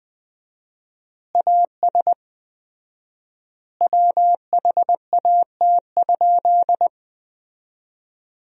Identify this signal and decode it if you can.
Morse code, heard as a keyed tone, about 20 wpm, 700 Hz. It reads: AS WHAT?
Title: 4x Word-Spacing - Sentences from Top 200 Words - 20wpm